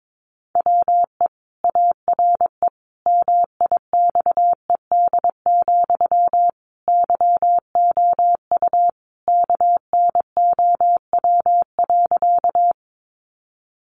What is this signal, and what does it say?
Morse code, heard as a keyed tone, about 22 wpm, 700 Hz. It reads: WE ARE MIXED, YOU KNOW.
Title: Sentences from Top 1500 Words 22wpm